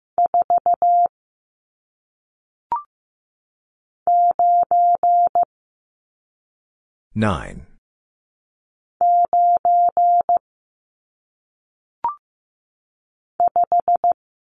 Morse code, heard as a keyed tone, about 15 wpm, 700 Hz. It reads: 4 9 9 5
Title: Numbers - 1-Digit 15wpm